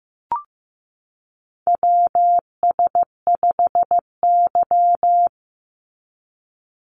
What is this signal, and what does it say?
Morse code, heard as a keyed tone, about 15 wpm, 700 Hz. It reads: WS5Y
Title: USA Call Signs 15wpm